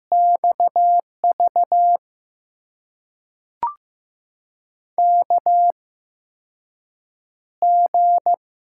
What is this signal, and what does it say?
Morse code, heard as a keyed tone, about 15 wpm, spaced as for 6 wpm, 700 Hz. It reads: XV KG
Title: USA Call Signs - 6 Characters - 8x Character Spacing - 15wpm